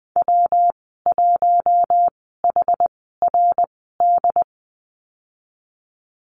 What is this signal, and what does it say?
Morse code, heard as a keyed tone, about 20 wpm, spaced as for 14 wpm, 700 Hz. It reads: W1HRD
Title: 167 Lesson 27g - Number 7 - Call Signs 20wpm